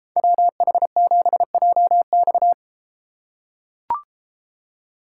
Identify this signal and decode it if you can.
Morse code, heard as a keyed tone, about 32 wpm, 700 Hz. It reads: WH7JX